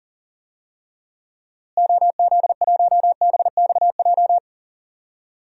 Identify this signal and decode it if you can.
Morse code, heard as a keyed tone, about 39 wpm, 700 Hz. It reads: OZ1BXJ
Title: DXCC-100 Call Signs - 6 Characters - Repeated 3x - 40wpm